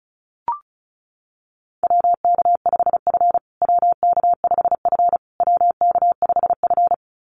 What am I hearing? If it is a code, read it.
Morse code, heard as a keyed tone, about 34 wpm, 700 Hz. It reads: WK5F WK5F WK5F
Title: USA Call Signs Repeated 3x 35wpm